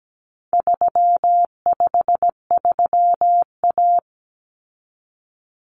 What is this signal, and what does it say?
Morse code, heard as a keyed tone, about 17 wpm, 700 Hz. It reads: 353A
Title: Common Cut Numbers - 4-Digit 17wpm